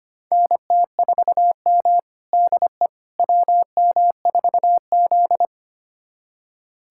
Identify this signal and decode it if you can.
Morse code, heard as a keyed tone, about 25 wpm, 700 Hz. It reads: NT4M DE WM4Z
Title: USA Callsign DE Callsign - 4 Characters - 25wpm